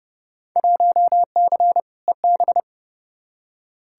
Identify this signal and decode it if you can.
Morse code, heard as a keyed tone, about 31 wpm, 700 Hz. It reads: 1C EB
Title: Field Day Exchanges Encore - 30wpm